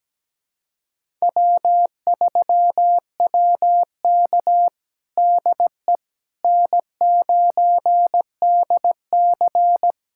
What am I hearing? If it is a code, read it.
Morse code, heard as a keyed tone, about 17 wpm, 700 Hz. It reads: W3WK DE N9DC
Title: USA Callsign DE Callsign - 4 Characters - 17wpm